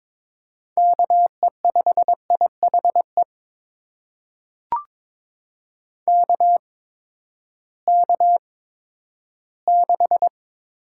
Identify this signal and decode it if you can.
Morse code, heard as a keyed tone, about 22 wpm, spaced as for 12 wpm, 700 Hz. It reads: KE5IHE KK6
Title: USA Call Signs - 6 Characters - 8x Character Spacing - 22wpm